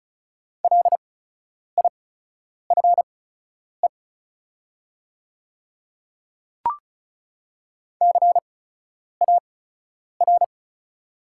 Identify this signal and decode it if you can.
Morse code, heard as a keyed tone, about 36 wpm, spaced as for 10 wpm, 700 Hz. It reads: LIFE CAR
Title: 8x Character Spacing - Top 300 Words - 35wpm